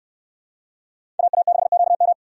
Morse code, heard as a keyed tone, about 66 wpm, 690 Hz. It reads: AA6QM